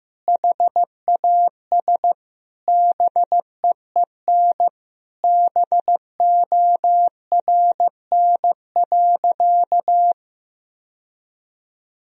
Morse code, heard as a keyed tone, about 16 wpm, 700 Hz. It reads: HAS BEEN BORN.
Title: Sentences from Top 1000 Words - Review Part 2 - 15wpm